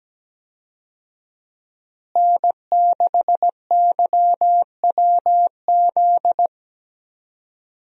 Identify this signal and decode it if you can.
Morse code, heard as a keyed tone, about 17 wpm, 700 Hz. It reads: N6YWZ
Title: USA Call Signs - 5 Characters - Repeated 2x - 17wpm